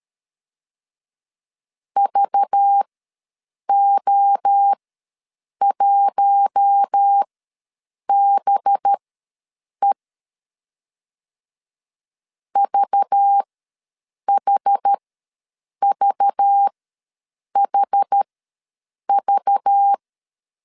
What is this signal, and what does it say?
Morse code, heard as a keyed tone, about 13 wpm, spaced as for 7 wpm, 800 Hz. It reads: VO1BE VHVHV
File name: LEZIONI / 5 LBVC / 5 LBV ws 12wpm 3015